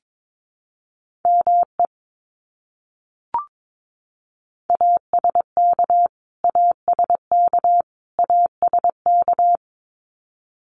Morse code, heard as a keyed tone, about 22 wpm, 700 Hz. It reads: ME ASK ASK ASK
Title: Top 200 Words Repeated 22wpm